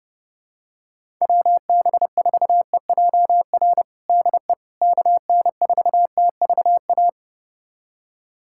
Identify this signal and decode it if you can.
Morse code, heard as a keyed tone, about 32 wpm, 700 Hz. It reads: WB4EJR DE KN4TVA
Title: USA Callsign DE Callsign 30wpm